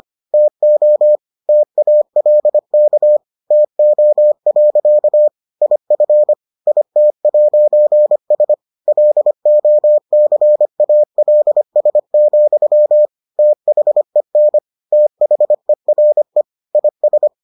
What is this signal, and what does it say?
Morse code, heard as a keyed tone, about 25 wpm, 600 Hz. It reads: TO TALK TO. IF IT'S LOCALS, THEN THERE IS